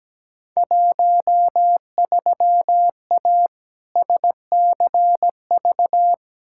Morse code, heard as a keyed tone, about 17 wpm, 700 Hz. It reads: 13A SCV
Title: Field Day Exchanges Encore - 17wpm